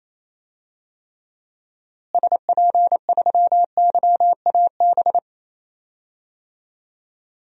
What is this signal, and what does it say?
Morse code, heard as a keyed tone, about 28 wpm, 700 Hz. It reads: SP3YAB
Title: DXCC-100 Call Signs - 6 Characters - Repeated 2x - 28wpm